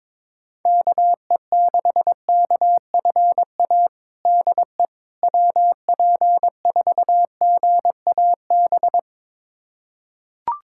Morse code, heard as a keyed tone, about 22 wpm, 700 Hz. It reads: KE6KFA DE WP4GAB
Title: USA Callsign DE Callsign - 6 Characters - 22wpm